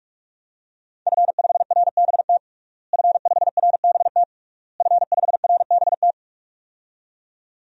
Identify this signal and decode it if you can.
Morse code, heard as a keyed tone, about 46 wpm, 700 Hz. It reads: F5RBT F5RBT F5RBT
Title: DXCC-100 Call Signs - 5 Characters - Repeated 3x - 45wpm